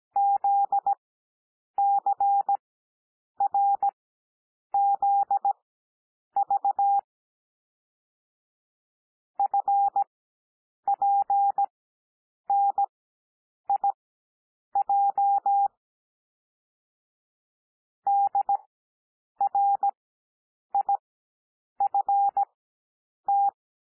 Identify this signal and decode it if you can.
Morse code, heard as a keyed tone, about 17 wpm, spaced as for 8 wpm, 800 Hz. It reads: ZCRZV FPNIJ DRIFT
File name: cwmp3 / L2 SOUND LETTERS 17 wpm